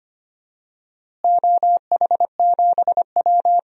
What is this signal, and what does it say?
Morse code, heard as a keyed tone, about 25 wpm, 700 Hz. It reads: OH7W
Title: DXCC-100 Call Signs - 4 Characters - Repeated 2x - 25wpm